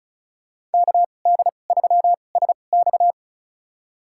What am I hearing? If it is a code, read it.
Morse code, heard as a keyed tone, about 36 wpm, spaced as for 25 wpm, 700 Hz. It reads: KD3SX
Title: USA Call Signs - 5 Characters - 2x Character Spacing - 35wpm